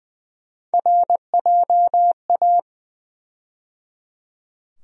Morse code, heard as a keyed tone, about 21 wpm, 700 Hz. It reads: RJA
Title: USA Call Sign Postfixes - 3 Letters 20wpm